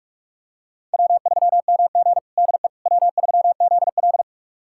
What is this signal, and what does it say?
Morse code, heard as a keyed tone, about 47 wpm, 700 Hz. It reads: W3MG DE W3ZL
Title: USA Callsign DE Callsign - 4 Characters - 45wpm